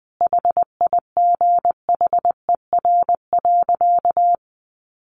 Morse code, heard as a keyed tone, about 20 wpm, 700 Hz. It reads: HIGHER.
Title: Additional Easy Phrases 20wpm